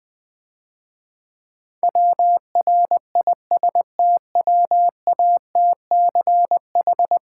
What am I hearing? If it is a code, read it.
Morse code, heard as a keyed tone, about 20 wpm, 700 Hz. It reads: WRISTWATCH
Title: Difficult Words - Speed-Racing with Hints - 20wpm